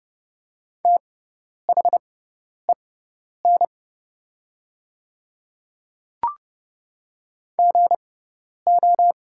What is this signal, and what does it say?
Morse code, heard as a keyed tone, about 30 wpm, spaced as for 10 wpm, 690 Hz. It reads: THEN GO
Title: 6x Character Spacing - Top 100 Words - 30wpm